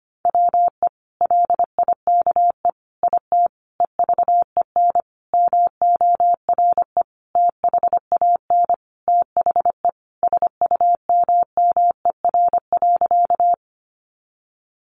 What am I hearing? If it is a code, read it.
Morse code, heard as a keyed tone, about 25 wpm, 700 Hz. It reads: WE LIKE IT EVEN MORE THAN THE SUMMER.
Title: Sentences from Top 1100 Words 25wpm